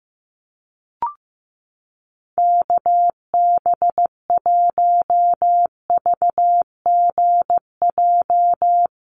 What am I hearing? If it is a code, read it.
Morse code, heard as a keyed tone, about 15 wpm, 700 Hz. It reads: KB1VGJ
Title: USA Call Signs Repeated 2x 15wpm